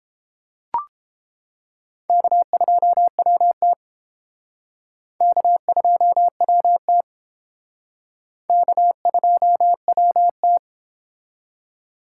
Morse code, heard as a keyed tone, about 28 wpm, 700 Hz. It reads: K2WT K2WT K2WT